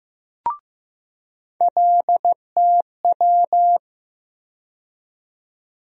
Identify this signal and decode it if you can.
Morse code, heard as a keyed tone, about 15 wpm, 700 Hz. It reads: LTW
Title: USA Call Sign Postfixes - 3 Letters 15wpm